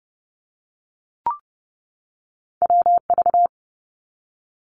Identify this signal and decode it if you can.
Morse code, heard as a keyed tone, about 30 wpm, 700 Hz. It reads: WV
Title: US States 30wpm